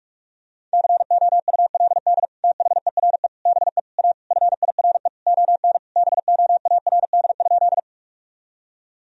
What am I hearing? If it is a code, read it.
Morse code, heard as a keyed tone, about 47 wpm, 700 Hz. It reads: COULD THERE BE A FIRE ON BOARD?
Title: Sentences from Top 1000 Words - Review Part 4 - 45wpm